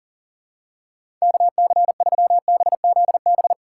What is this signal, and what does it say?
Morse code, heard as a keyed tone, about 40 wpm, 700 Hz. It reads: KC3BZB
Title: USA Call Signs - 6 Characters - 6x Character Spacing - 40wpm